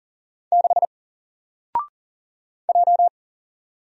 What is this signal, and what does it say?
Morse code, heard as a keyed tone, about 40 wpm, 700 Hz. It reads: 6 J
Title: Single Character (Letter:Number:Punctuation:Prosign) 40wpm